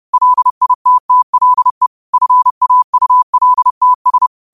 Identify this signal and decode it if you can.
Morse code, heard as a keyed tone, about 30 wpm, 1000 Hz. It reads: LITTLE FAULTS